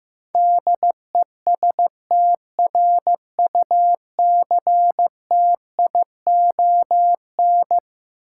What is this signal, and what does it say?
Morse code, heard as a keyed tone, about 15 wpm, 700 Hz. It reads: DESTRUCTION
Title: Sets of 5 Words Encore - New Word Drill - 15wpm